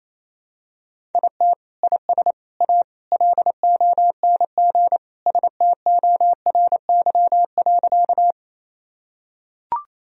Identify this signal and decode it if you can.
Morse code, heard as a keyed tone, about 28 wpm, 700 Hz. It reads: IT IS A LONG STORY.
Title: Sentences from Top 1000 Words - Review Part 1 - 28wpm